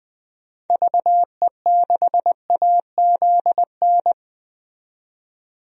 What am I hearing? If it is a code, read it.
Morse code, heard as a keyed tone, about 20 wpm, 700 Hz. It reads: VE6AZN
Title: DXCC-100 Call Signs - 6 Characters - 20wpm